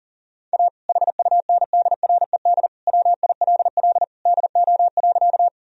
Morse code, heard as a keyed tone, about 40 wpm, 700 Hz. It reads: A HUNDRED WILL DO.
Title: Sentences from Top 1000 Words - Review Part 2 - 40wpm